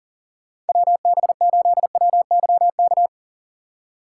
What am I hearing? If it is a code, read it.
Morse code, heard as a keyed tone, about 50 wpm, 700 Hz. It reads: WB8WYK
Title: Speed Racing - USA 2x3 Call Signs 40wpm